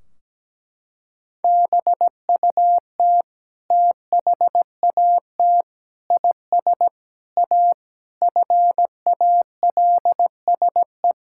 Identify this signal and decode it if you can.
Morse code, heard as a keyed tone, about 17 wpm, 700 Hz. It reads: BUT THAT IS A FALSE
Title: Sentences from Top 1600 Words 17wpm